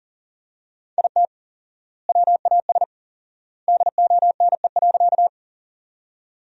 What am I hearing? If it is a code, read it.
Morse code, heard as a keyed tone, about 40 wpm, 700 Hz. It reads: IT WAS DONE.
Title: 4x Word-Spacing - Sentences from Top 500 Words - 40wpm